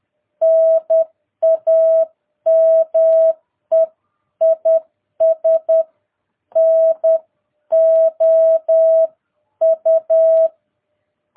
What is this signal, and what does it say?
Morse code, heard as a keyed tone, about 10 wpm, 660 Hz. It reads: NAME IS NOU
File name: jh8cbh / 2022cw